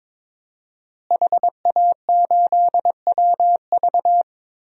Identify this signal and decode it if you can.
Morse code, heard as a keyed tone, about 22 wpm, 700 Hz. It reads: HA8WV
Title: DXCC-100 Call Signs - 5 Characters - Repeated 2x - 22wpm